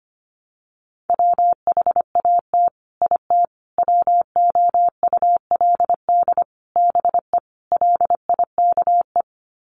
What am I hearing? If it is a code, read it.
Morse code, heard as a keyed tone, about 25 wpm, 700 Hz. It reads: WHAT IT WOULD BE LIKE
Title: Sets of 5 Words Encore 25wpm